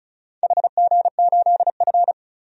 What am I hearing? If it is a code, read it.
Morse code, heard as a keyed tone, about 36 wpm, 700 Hz. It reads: HG8F
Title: DXCC-100 Call Signs - 4 Characters - 35wpm